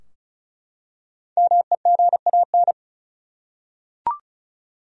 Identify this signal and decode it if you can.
Morse code, heard as a keyed tone, about 36 wpm, 700 Hz. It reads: MEGAN
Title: USA Common Names - Max 5 Letters - 35wpm